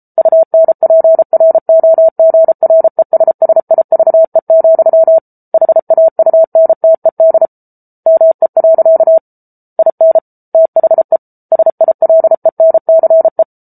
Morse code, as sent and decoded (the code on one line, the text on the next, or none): ..- -. .--. .-. --- --. .-. ... ... .. ...- . --..-- / .... .- ..- -. - . -.. / -- . .-.-.- / .. -. / - .... / ... .. .-.. -. -.-. .
UNPROGRESSIVE, HAUNTED ME. IN THE SILENCE